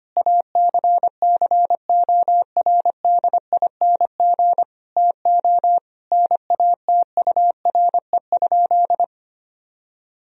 Morse code, heard as a keyed tone, about 25 wpm, 700 Hz. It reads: ACCORDING TO NATURE?
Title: Sentences from Top 1000 Words - Review Part 4 - 25wpm